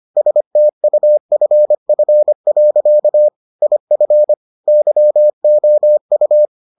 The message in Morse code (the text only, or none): STUFF. IF YOU